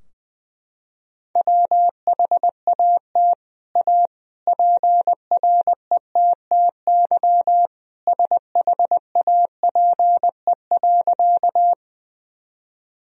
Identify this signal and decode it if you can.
Morse code, heard as a keyed tone, about 20 wpm, 700 Hz. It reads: WHAT A PRETTY SHAPE.
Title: Sentences from Top 1000 Words - Review Part 3 - 20wpm